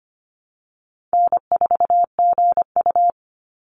Morse code, heard as a keyed tone, about 25 wpm, 700 Hz. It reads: N4GU
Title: Call Signs - 25wpm